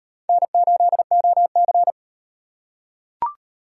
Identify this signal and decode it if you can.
Morse code, heard as a keyed tone, about 37 wpm, 700 Hz. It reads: N8OC